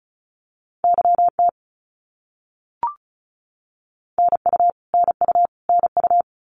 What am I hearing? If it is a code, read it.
Morse code, heard as a keyed tone, about 34 wpm, 700 Hz. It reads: YT NU NU NU
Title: Canadian Provinces and Territories - Repeated 3x - 35wpm